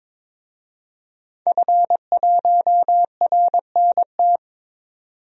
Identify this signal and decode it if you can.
Morse code, heard as a keyed tone, about 22 wpm, 700 Hz. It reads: F1RNT
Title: DXCC-100 Call Signs - 5 Characters - 22wpm